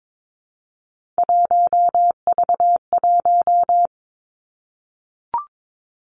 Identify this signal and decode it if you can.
Morse code, heard as a keyed tone, about 22 wpm, 700 Hz. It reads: 1V1
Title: Three Challenging Letter-Number Combinations 22wpm